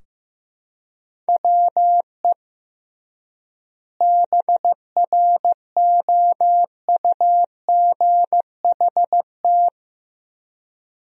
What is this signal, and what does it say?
Morse code, heard as a keyed tone, about 15 wpm, 700 Hz. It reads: WE BROUGHT
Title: 3x Word-Spacing - Sentences from Top 500 Words - 15wpm